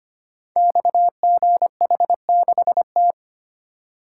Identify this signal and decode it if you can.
Morse code, heard as a keyed tone, about 25 wpm, 700 Hz. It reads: XGH6T
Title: Code Groups - 5 Letter:Numbers - 25wpm